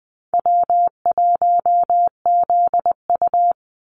Morse code, heard as a keyed tone, about 19 wpm, 700 Hz. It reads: W1ZU
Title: NA SOTA Activator Callsigns 20wpm